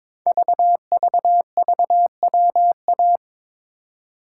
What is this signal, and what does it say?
Morse code, heard as a keyed tone, about 22 wpm, 700 Hz. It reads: VVVWA
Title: Code Groups - 5 Letters - 22wpm